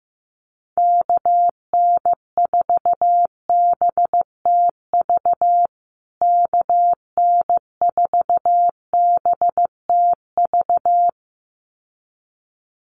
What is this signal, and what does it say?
Morse code, heard as a keyed tone, about 15 wpm, 700 Hz. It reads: KN4BTV KN4BTV
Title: USA Call Signs Repeated 2x 15wpm